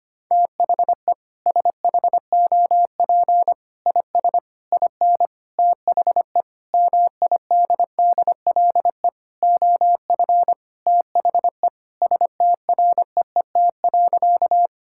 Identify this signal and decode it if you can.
Morse code, heard as a keyed tone, about 25 wpm, 700 Hz. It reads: THE SHOP IS IN THE MIDDLE OF THE STREET.
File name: practice / 25wpm / Sentences with 4-Word Phrases - 25wpm